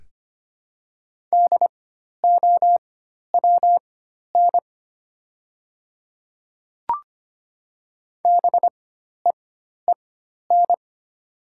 Morse code, heard as a keyed tone, about 25 wpm, spaced as for 12 wpm, 700 Hz. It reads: DOWN BEEN
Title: 4x Character Spacing - Top 100 Words - 25wpm